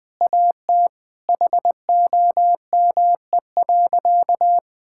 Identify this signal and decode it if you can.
Morse code, heard as a keyed tone, about 21 wpm, 700 Hz. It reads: AT HOME.
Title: Sentences from Top 1000 Words - Review Part 6 - 20wpm